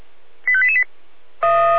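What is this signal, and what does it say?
Five-tone selective call (ZVEI), heard as quick runs of five tones, tones of about 70 ms. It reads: 85888
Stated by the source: ZVEI five-tone selective call received on an SDR.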